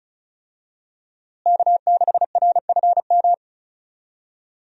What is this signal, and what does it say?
Morse code, heard as a keyed tone, about 34 wpm, 700 Hz. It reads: K6RFM
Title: USA Call Signs - 5 Characters - Repeated 2x - 35wpm